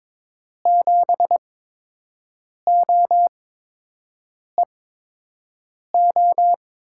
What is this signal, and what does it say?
Morse code, heard as a keyed tone, about 22 wpm, spaced as for 6 wpm, 700 Hz. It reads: 7OEO
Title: USA Call Signs - 6 Characters - 8x Character Spacing - 22wpm